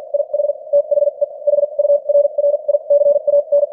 Morse code, heard as a keyed tone, about 50 wpm, spaced as for 39 wpm, 580 Hz. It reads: IH THE HURRICAN
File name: archived / CWBros